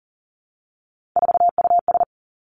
Morse code, heard as a keyed tone, about 39 wpm, 700 Hz. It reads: 4US
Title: Three Challenging Letter-Number Combinations 40wpm